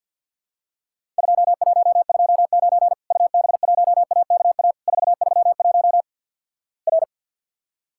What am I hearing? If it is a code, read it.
Morse code, heard as a keyed tone, about 54 wpm, 700 Hz. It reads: 2129 UB1AKA 431 R
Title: Worked All Europe DX Contest - QTC Exchange with Repeats - 50wpm